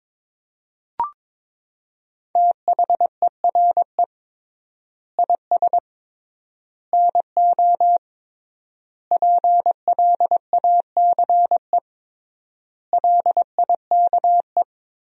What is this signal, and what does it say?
Morse code, heard as a keyed tone, about 22 wpm, 700 Hz. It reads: THERE IS NO PLACE LIKE
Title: 3x Word-Spacing - Sentences from Top 200 Words - 22wpm